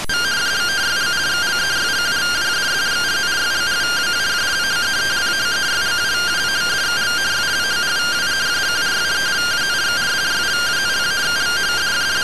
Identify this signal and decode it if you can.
Two-tone teleprinter signal, about 170 Hz shift.